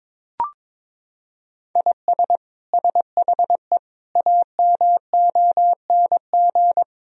Morse code, heard as a keyed tone, about 22 wpm, 700 Hz. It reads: IS SHE AMONG
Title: Sentences from Top 1000 Words - Review Part 3 - 22wpm